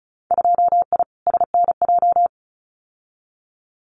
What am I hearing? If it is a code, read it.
Morse code, heard as a keyed tone, about 36 wpm, 700 Hz. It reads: 2I SNJ
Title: Winter Field Day Exchanges Encore 35wpm